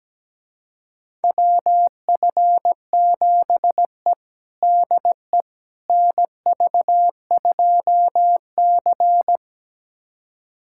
Morse code, heard as a keyed tone, about 17 wpm, 700 Hz. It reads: WF7E DE NV2C
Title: USA Callsign DE Callsign - 4 Characters - 17wpm